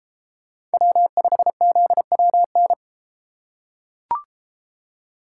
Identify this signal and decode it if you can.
Morse code, heard as a keyed tone, about 32 wpm, 700 Hz. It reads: W5ZWN